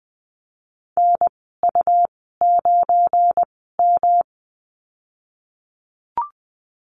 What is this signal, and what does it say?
Morse code, heard as a keyed tone, about 20 wpm, spaced as for 15 wpm, 700 Hz. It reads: NU9M
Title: 179 Lesson 29g - Slash - Call Signs 20wpm